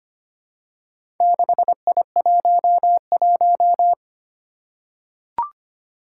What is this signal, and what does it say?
Morse code, heard as a keyed tone, about 25 wpm, 700 Hz. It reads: 6I11
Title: Four Challenging Letter-Number Combinations 25wpm